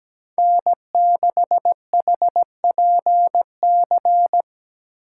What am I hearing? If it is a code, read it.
Morse code, heard as a keyed tone, about 17 wpm, 700 Hz. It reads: N6HPC
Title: USA Call Signs - 5 Characters - 2x Character Spacing - 17wpm